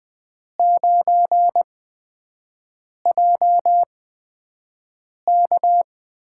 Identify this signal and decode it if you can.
Morse code, heard as a keyed tone, about 20 wpm, spaced as for 5 wpm, 700 Hz. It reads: 9JK
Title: USA Call Signs - 8x Character Spacing - 20wpm